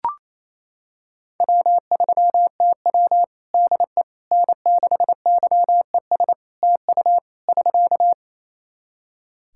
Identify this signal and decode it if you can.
Morse code, heard as a keyed tone, about 28 wpm, 700 Hz. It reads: W3TW DE N6YES TU <SK>
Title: Practice QSO - Exchange 4 - Ending - 28wpm